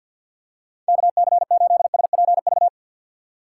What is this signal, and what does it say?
Morse code, heard as a keyed tone, about 53 wpm, 700 Hz. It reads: KC8SPV
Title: USA Call Signs - 2x Character Spacing - 50wpm